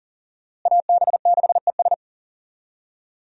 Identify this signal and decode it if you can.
Morse code, heard as a keyed tone, about 40 wpm, 700 Hz. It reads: AB6ES
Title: USA Call Signs - 5 Characters - Repeated 2x - 40wpm